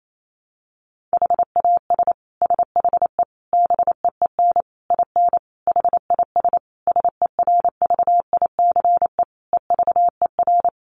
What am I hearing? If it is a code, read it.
Morse code, heard as a keyed tone, about 28 wpm, 700 Hz. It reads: HAS SHE BEEN IN HIS SERVICE EVER
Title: Sentences from Top 700 Words 28wpm